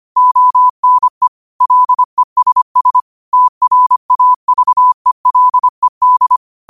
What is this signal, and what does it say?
Morse code, heard as a keyed tone, about 25 wpm, 1000 Hz. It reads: ONE LESS TRAVELED
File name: qotdfiles / 25wpm